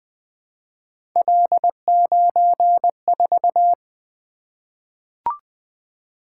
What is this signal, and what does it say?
Morse code, heard as a keyed tone, about 20 wpm, 700 Hz. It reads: L94
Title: Three Letter-Number Combinations - 20wpm